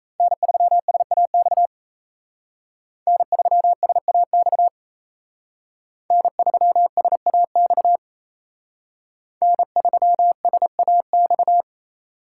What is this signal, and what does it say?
Morse code, heard as a keyed tone, about 32 wpm, 700 Hz. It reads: N3SAX N3SAX N3SAX N3SAX
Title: Speed Racing - USA 1x3 Call Signs 28wpm